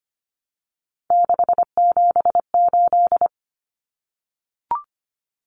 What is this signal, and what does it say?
Morse code, heard as a keyed tone, about 25 wpm, 700 Hz. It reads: 678
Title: Common Cut Numbers - 3-Digit 25wpm